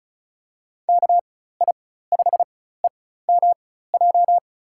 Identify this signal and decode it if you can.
Morse code, heard as a keyed tone, about 36 wpm, spaced as for 16 wpm, 700 Hz. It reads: KI5EMJ
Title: USA Call Signs - 4x Character Spacing - 35wpm